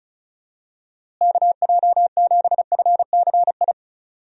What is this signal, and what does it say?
Morse code, heard as a keyed tone, about 34 wpm, 700 Hz. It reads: KJ7FCI
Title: USA Call Signs - 6 Characters - Repeated 3x - 35wpm